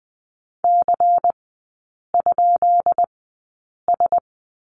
Morse code, heard as a keyed tone, about 20 wpm, spaced as for 8 wpm, 700 Hz. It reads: C?S
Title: 133 Lesson 22e - Question Mark - Three Characters 20wpm